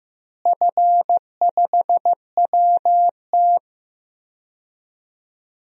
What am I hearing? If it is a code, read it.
Morse code, heard as a keyed tone, about 15 wpm, 700 Hz. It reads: F5WT
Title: DXCC-100 Call Signs - 4 Characters - Repeated 2x - 15wpm